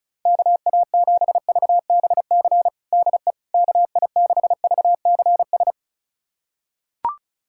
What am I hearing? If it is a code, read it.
Morse code, heard as a keyed tone, about 34 wpm, 700 Hz. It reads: KA7VBC DE KI6VCS
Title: USA Callsign DE Callsign - 6 Characters - 35wpm